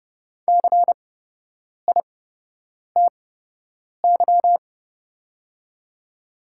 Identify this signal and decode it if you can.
Morse code, heard as a keyed tone, about 30 wpm, spaced as for 8 wpm, 700 Hz. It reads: CITY
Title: 8x Character Spacing - Top 300 Words - 30wpm